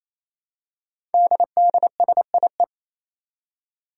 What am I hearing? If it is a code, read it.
Morse code, heard as a keyed tone, about 27 wpm, 700 Hz. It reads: DDSIE
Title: Five Challenging Letter-Number Combinations 28wpm